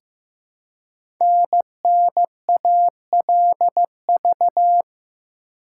Morse code, heard as a keyed tone, about 15 wpm, 700 Hz. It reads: NNALV
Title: Code Groups - 5 Letter:Numbers - 15wpm